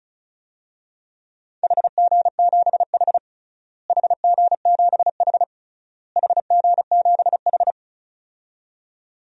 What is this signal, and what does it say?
Morse code, heard as a keyed tone, about 36 wpm, 700 Hz. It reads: HG7H HG7H HG7H
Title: DXCC-100 Call Signs - 4 Characters - Repeated 3x - 35wpm